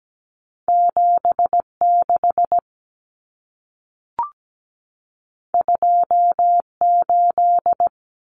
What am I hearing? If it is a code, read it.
Morse code, heard as a keyed tone, about 17 wpm, 700 Hz. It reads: 76 28
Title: Numbers - 2-Digit 17wpm